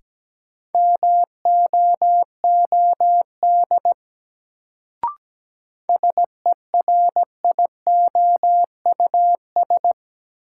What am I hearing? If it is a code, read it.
Morse code, heard as a keyed tone, about 17 wpm, 700 Hz. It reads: MOOD SERIOUS
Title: Sentences from Top 1100 Words - New Word Drill - 17wpm